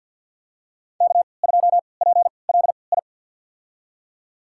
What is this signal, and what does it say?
Morse code, heard as a keyed tone, about 49 wpm, spaced as for 28 wpm, 700 Hz. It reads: K2PLI
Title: USA Call Signs - 3x Character Spacing - 50wpm